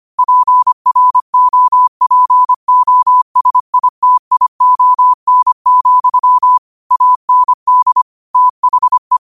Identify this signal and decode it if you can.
Morse code, heard as a keyed tone, about 25 wpm, 1000 Hz. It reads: PROPOSITION, AND THE
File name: qotdfiles / 25wpm